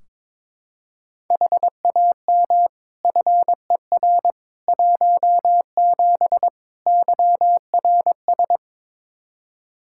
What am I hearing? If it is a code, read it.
Morse code, heard as a keyed tone, about 21 wpm, 700 Hz. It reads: HAM FER 17 YRS
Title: Practice QSO - Exchange 3 - More Info - 22wpm